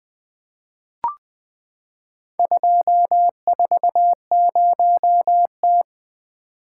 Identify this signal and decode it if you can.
Morse code, heard as a keyed tone, about 20 wpm, 700 Hz. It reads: 240T
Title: Four Letter-Number Combinations - 20wpm